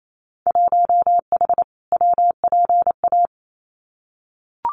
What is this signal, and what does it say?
Morse code, heard as a keyed tone, about 28 wpm, 700 Hz. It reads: 1H WPA
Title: Winter Field Day Exchanges 28wpm